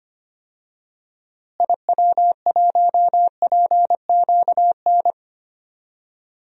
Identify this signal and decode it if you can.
Morse code, heard as a keyed tone, about 25 wpm, 700 Hz. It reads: IW1PQN
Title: DXCC-100 Call Signs - 6 Characters - Repeated 2x - 25wpm